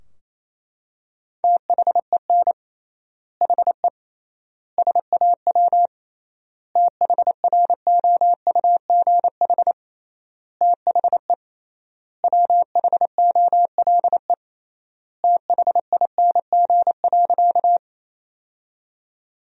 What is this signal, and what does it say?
Morse code, heard as a keyed tone, about 28 wpm, 700 Hz. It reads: THEN HE SAW THROUGH THE WHOLE THING.
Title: 3x Word-Spacing - Sentences from Top 400 Words - 28wpm